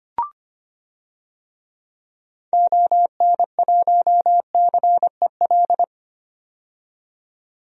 Morse code, heard as a keyed tone, about 25 wpm, 700 Hz. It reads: ON1CEL
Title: DXCC-100 Call Signs - 6 Characters - 25wpm